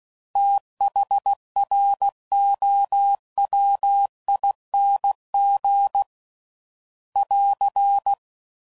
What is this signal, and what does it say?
Morse code, heard as a keyed tone, about 16 wpm, 800 Hz. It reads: THROWING <AR>
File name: Prefixes / FLs & THs / TH Prefixes 3